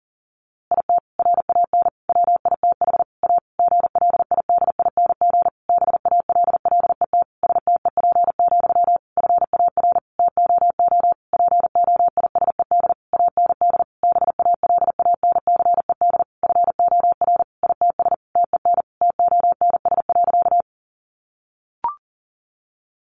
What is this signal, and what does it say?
Morse code, heard as a keyed tone, about 37 wpm, 700 Hz. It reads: IT RAN WITH A GLIDING BALLET STEP, FAR TOO POISED AND BALANCED FOR ITS TEN TONS.
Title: A Sound Of Thunder 40wpm